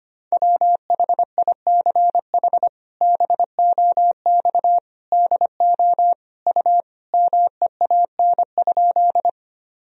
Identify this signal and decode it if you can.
Morse code, heard as a keyed tone, about 25 wpm, 700 Hz. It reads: WHICH BOX DO U MEAN?
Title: Sentences from Top 500 Words and Abbreviations 25wpm